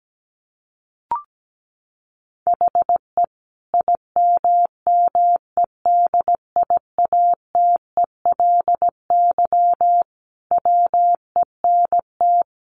Morse code, heard as a keyed tone, about 17 wpm, 700 Hz. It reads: HE IMMEDIATELY WENT
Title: Sentences from Top 1100 Words 17wpm